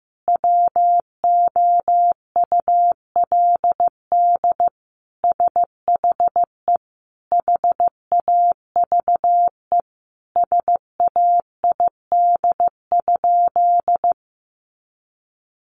Morse code, heard as a keyed tone, about 15 wpm, 700 Hz. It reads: WOULD SHE HAVE SAID?
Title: Sentences from Top 100 Words 15wpm